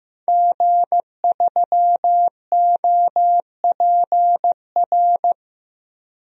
Morse code, heard as a keyed tone, about 15 wpm, 700 Hz. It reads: G3OPR
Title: DXCC-100 Call Signs - 5 Characters - 15wpm